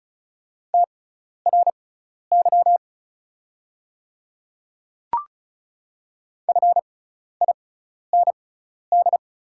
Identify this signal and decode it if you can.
Morse code, heard as a keyed tone, about 36 wpm, spaced as for 12 wpm, 700 Hz. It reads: TRY FIND
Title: 6x Character Spacing - Top 300 Words - 35wpm